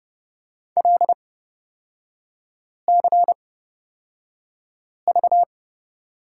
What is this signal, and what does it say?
Morse code, heard as a keyed tone, about 29 wpm, spaced as for 5 wpm, 700 Hz. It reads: LCV
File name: Lesson 37e - Letter x - Three Characters 30wpm